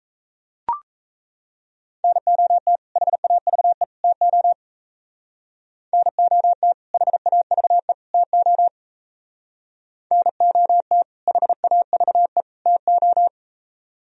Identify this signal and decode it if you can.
Morse code, heard as a keyed tone, about 35 wpm, 700 Hz. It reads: NOT HAVE TO NOT HAVE TO NOT HAVE TO